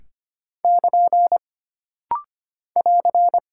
Morse code, heard as a keyed tone, about 25 wpm, 700 Hz. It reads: <KN> <AR>
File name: Single Punctuation-Prosign - Rapid-Fire - 25wpm